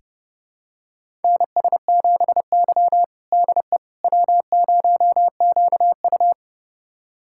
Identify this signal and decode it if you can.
Morse code, heard as a keyed tone, about 30 wpm, 700 Hz. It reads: NS7Y DE W0QU
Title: USA Callsign DE Callsign - 4 Characters - 30wpm